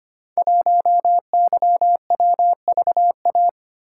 Morse code, heard as a keyed tone, about 25 wpm, 700 Hz. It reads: 1YWVA
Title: Code Groups - 5 Letter:Numbers - 25wpm